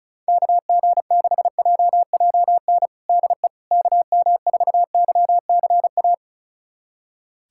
Morse code, heard as a keyed tone, about 34 wpm, 700 Hz. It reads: KG6JJN DE KM4YCA
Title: USA Callsign DE Callsign - 6 Characters - 35wpm